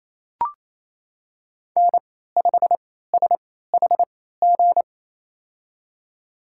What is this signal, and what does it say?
Morse code, heard as a keyed tone, about 27 wpm, spaced as for 16 wpm, 700 Hz. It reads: N5SHG
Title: USA Call Signs - 3x Character Spacing - 28wpm